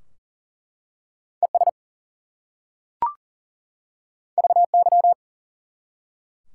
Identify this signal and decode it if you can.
Morse code, heard as a keyed tone, about 40 wpm, 700 Hz. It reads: ES VY
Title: Top 90 QSO Elements 40wpm